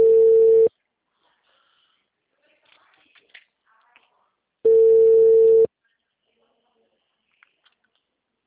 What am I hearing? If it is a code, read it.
Ringback tone (China, 450 Hz).
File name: out- -2022- - - .18960